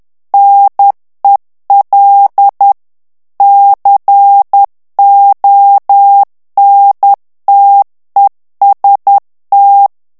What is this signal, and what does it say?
Morse code, heard as a keyed tone, about 10 wpm, 800 Hz. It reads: NEL CONTEST
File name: ITALIEN texte clair 2 vitesse 10 mots minute